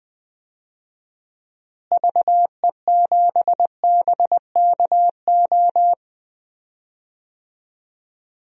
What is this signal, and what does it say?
Morse code, heard as a keyed tone, about 20 wpm, 700 Hz. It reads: VE7BKO
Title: DXCC-100 Call Signs - 6 Characters - Repeated 3x - 20wpm